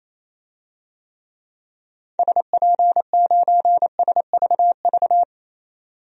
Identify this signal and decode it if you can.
Morse code, heard as a keyed tone, about 28 wpm, 700 Hz. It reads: SP9SVV
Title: DXCC-100 Call Signs - 6 Characters - 28wpm